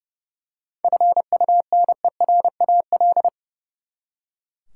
Morse code, heard as a keyed tone, about 30 wpm, 700 Hz. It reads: FUNERAL
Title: Seven Letter Words - 30wpm